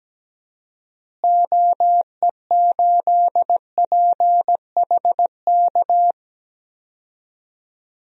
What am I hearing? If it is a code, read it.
Morse code, heard as a keyed tone, about 17 wpm, 700 Hz. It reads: OE8PHK
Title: DXCC-100 Call Signs - 6 Characters - Repeated 3x - 17wpm